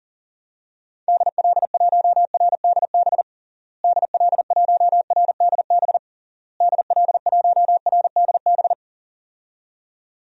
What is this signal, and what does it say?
Morse code, heard as a keyed tone, about 40 wpm, 700 Hz. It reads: DL1RDB DL1RDB DL1RDB
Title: DXCC-100 Call Signs - 6 Characters - Repeated 3x - 40wpm